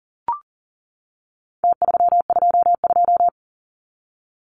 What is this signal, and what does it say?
Morse code, heard as a keyed tone, about 40 wpm, 700 Hz. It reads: T322
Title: Numbers - Contest Style - 4-Digit 40wpm